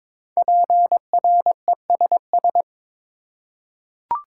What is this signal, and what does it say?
Morse code, heard as a keyed tone, about 22 wpm, 700 Hz. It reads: PRESS
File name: 5 Letter Words - First 3 Letters Given - 22wpm